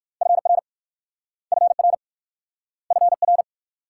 Morse code, heard as a keyed tone, about 55 wpm, 690 Hz. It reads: FR FR FR
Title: Speed-Racing Multi-letter Phonemes - 45wpm